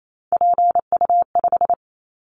Morse code, heard as a keyed tone, about 28 wpm, 700 Hz. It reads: PU5
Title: Callsign Prefixes for Top 100 DXCC - 28wpm